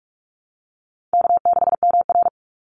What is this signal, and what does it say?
Morse code, heard as a keyed tone, about 44 wpm, 700 Hz. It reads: K6MR
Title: Easy Call Signs Repeated 3x - 45wpm